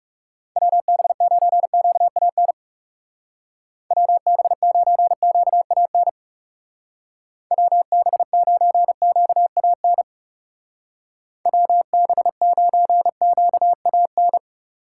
Morse code, heard as a keyed tone, about 36 wpm, 700 Hz. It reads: WB9QAN WB9QAN WB9QAN WB9QAN
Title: Speed Racing - USA 2x3 Call Signs 30wpm